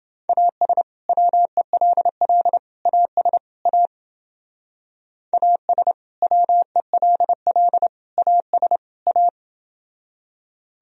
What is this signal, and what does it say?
Morse code, heard as a keyed tone, about 29 wpm, 700 Hz. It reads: AS WELL AS A AS WELL AS A